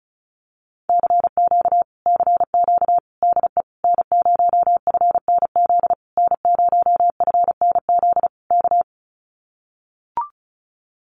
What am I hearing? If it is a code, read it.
Morse code, heard as a keyed tone, about 34 wpm, 700 Hz. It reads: CQ CQ DE N0FNZ N0FNZ K
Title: Common QSO Phrases Encore 35wpm